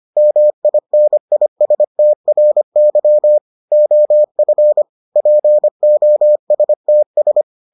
Morse code, heard as a keyed tone, about 25 wpm, 600 Hz. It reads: MINISTRY OF POSTS